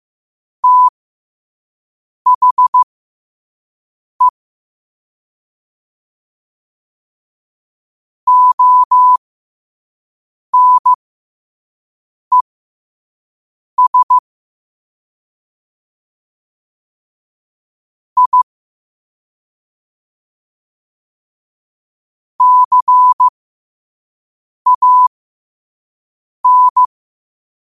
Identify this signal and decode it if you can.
Morse code, heard as a keyed tone, about 15 wpm, spaced as for 5 wpm, 1000 Hz. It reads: THE ONES I CAN
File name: qotdfiles / 5wpm